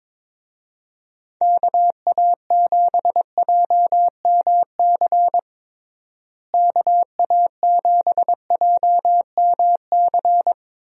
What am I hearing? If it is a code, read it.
Morse code, heard as a keyed tone, about 22 wpm, 700 Hz. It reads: KA7JMC KA7JMC
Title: USA Call Signs - 6 Characters - Repeated 2x - 22wpm